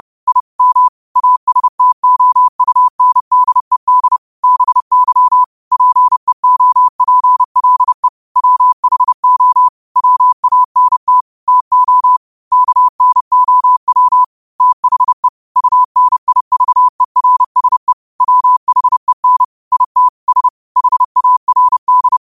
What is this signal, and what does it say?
Morse code, heard as a keyed tone, about 30 wpm, 1000 Hz. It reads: I M ASTOUNDED BY PEOPLE WHO WANT TO KNOW THE UNIVERSE WHEN IT S HARD
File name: qotdfiles / 30wpm